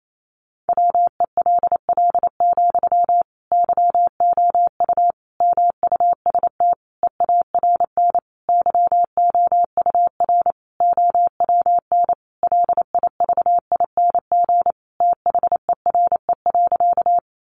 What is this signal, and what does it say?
Morse code, heard as a keyed tone, about 28 wpm, 700 Hz. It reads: WELL, YOU MUST EARN YOUR OWN LIVING THERE.
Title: Sentences from Top 1500 Words 28wpm